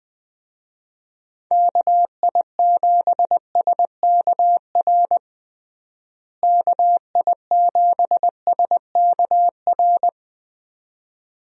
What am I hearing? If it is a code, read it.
Morse code, heard as a keyed tone, about 21 wpm, 700 Hz. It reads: KI7SKR KI7SKR
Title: USA Call Signs - 6 Characters - Repeated 2x - 20wpm